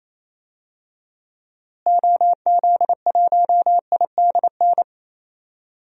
Morse code, heard as a keyed tone, about 27 wpm, 700 Hz. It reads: OZ1IDN
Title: DXCC-100 Call Signs - 6 Characters - Repeated 3x - 28wpm